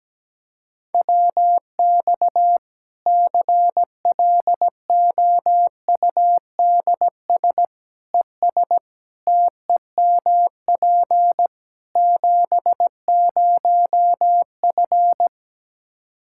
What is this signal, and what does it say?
Morse code, heard as a keyed tone, about 17 wpm, 700 Hz. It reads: WX CLOUDS ES TEMP 70F
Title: Practice QSO - Exchange 2 - Info - 17wpm